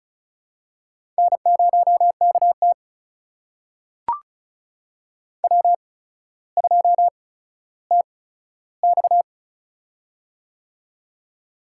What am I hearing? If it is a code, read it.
Morse code, heard as a keyed tone, about 36 wpm, spaced as for 15 wpm, 700 Hz. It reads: N0KT W2TX
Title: USA Call Signs - 4 Characters - 8x Character Spacing - 35wpm